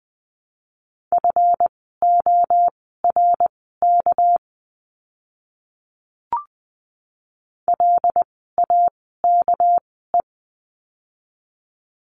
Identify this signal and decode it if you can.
Morse code, heard as a keyed tone, about 20 wpm, spaced as for 14 wpm, 700 Hz. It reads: FORK LAKE
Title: 199 Lesson 32f - Letter k - Words 20wpm